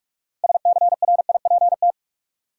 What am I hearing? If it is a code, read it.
Morse code, heard as a keyed tone, about 46 wpm, 700 Hz. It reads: SCRIPT
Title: Six Letter Words - 45wpm